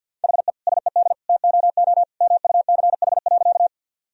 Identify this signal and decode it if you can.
Morse code, heard as a keyed tone, about 52 wpm, 700 Hz. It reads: HE SED TOO MUCH.
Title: Sentences from Top 200 Words and Abbreviations 50wpm